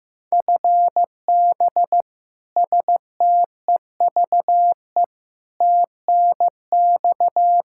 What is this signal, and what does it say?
Morse code, heard as a keyed tone, about 15 wpm, 700 Hz. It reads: FB STEVE TNX
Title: Practice QSO - Exchange 3 - More Info - 15wpm